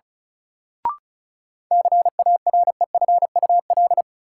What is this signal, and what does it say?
Morse code, heard as a keyed tone, about 34 wpm, 700 Hz. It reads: CAREFUL
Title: Sentences from Top 1600 Words - New Word Drill - 35wpm